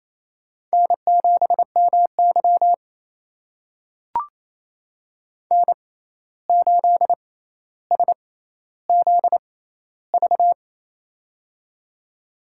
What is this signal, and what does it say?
Morse code, heard as a keyed tone, about 28 wpm, spaced as for 13 wpm, 700 Hz. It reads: N7MY N8SZV
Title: USA Call Signs - 6x Character Spacing - 28wpm